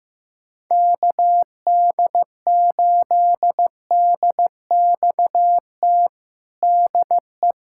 Morse code, heard as a keyed tone, about 15 wpm, 700 Hz. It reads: KD8DXT DE
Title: USA Callsign DE Callsign - 6 Characters - 15wpm